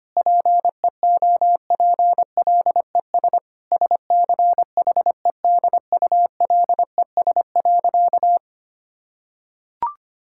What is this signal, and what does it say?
Morse code, heard as a keyed tone, about 25 wpm, 700 Hz. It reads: PEOPLES SCHEDULES.
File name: practice / 25wpm / Difficult Sentences with Hints - 25wpm